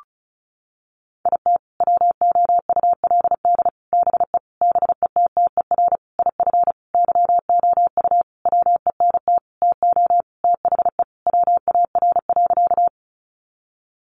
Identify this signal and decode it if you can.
Morse code, heard as a keyed tone, about 34 wpm, 700 Hz. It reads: IT WOULD BE BETTER IF YOU WENT TO THE WAR.
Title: Sentences from Top 400 Words 35wpm